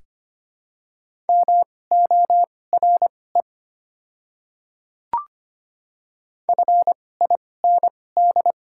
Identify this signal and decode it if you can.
Morse code, heard as a keyed tone, about 25 wpm, spaced as for 18 wpm, 700 Hz. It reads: MORE FIND
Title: 2x Character Spacing - Top 300 Words - 25wpm